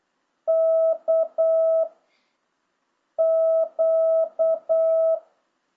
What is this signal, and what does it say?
Morse code, heard as a keyed tone, about 8 wpm, 650 Hz. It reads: K Q